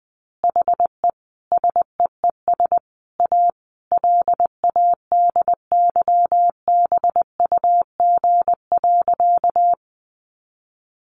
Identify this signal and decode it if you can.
Morse code, heard as a keyed tone, about 20 wpm, 700 Hz. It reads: HE SEES A LADYBUG.
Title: Additional Easy Phrases 20wpm